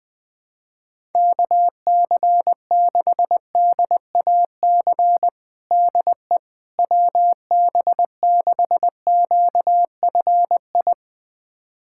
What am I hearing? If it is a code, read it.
Morse code, heard as a keyed tone, about 20 wpm, 700 Hz. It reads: KC6DAC DE WB6QFI
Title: USA Callsign DE Callsign - 6 Characters - 20wpm